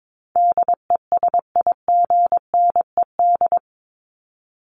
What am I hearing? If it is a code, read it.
Morse code, heard as a keyed tone, about 22 wpm, 700 Hz. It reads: DESIGNED
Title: ED Suffix - Top 100 Words with Base Words 22wpm